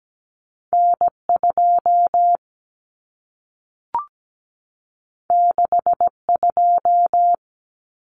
Morse code, heard as a keyed tone, about 17 wpm, 700 Hz. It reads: N2 62
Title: Common Cut Numbers - 2-Digit 17wpm